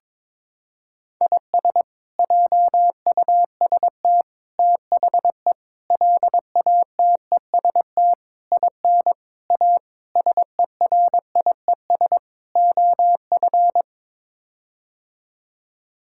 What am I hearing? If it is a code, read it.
Morse code, heard as a keyed tone, about 22 wpm, 700 Hz. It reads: IS JUST THE LATEST IN A SERIES OF
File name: practice / 22wpm / Sets of 8 Words - 22wpm